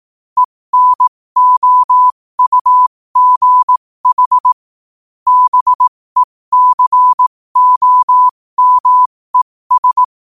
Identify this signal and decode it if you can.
Morse code, heard as a keyed tone, about 18 wpm, spaced as for 15 wpm, 1000 Hz. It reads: ENOUGH BECOMES
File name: qotdfiles / 15wpm